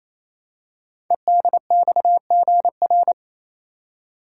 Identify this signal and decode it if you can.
Morse code, heard as a keyed tone, about 28 wpm, 700 Hz. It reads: EDXGR
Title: Code Groups - 5 Letters - 28wpm